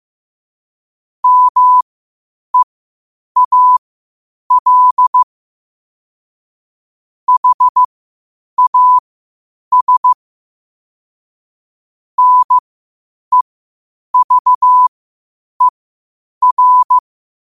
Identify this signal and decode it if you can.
Morse code, heard as a keyed tone, about 15 wpm, spaced as for 8 wpm, 1000 Hz. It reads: MEAL HAS NEVER
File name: qotdfiles / 8wpm